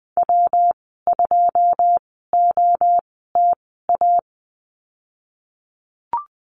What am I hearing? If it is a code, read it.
Morse code, heard as a keyed tone, about 20 wpm, spaced as for 14 wpm, 700 Hz. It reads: W2OTA
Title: 147 Lesson 24g - Letter y - Call Signs 20wpm